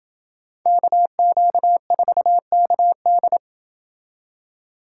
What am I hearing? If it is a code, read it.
Morse code, heard as a keyed tone, about 28 wpm, 700 Hz. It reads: KQ4KD